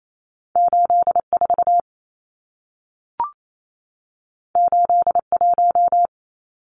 Morse code, heard as a keyed tone, about 28 wpm, 700 Hz. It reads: 84 81
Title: Numbers - 2-Digit 28wpm